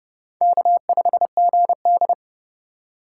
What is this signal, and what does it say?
Morse code, heard as a keyed tone, about 37 wpm, 700 Hz. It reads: K5GD
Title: Speed Racing - USA 1x2 Call Signs 30wpm